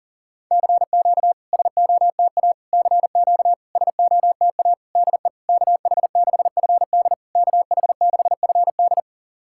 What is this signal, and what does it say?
Morse code, heard as a keyed tone, about 40 wpm, 690 Hz. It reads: CQ SOTA CQ SOTA DE KH6FD KH6FD
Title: SOTA QSO Practice 40wpm